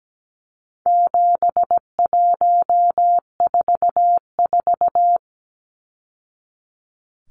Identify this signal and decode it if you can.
Morse code, heard as a keyed tone, about 17 wpm, 700 Hz. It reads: 7144
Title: Numbers - 4-Digit 17wpm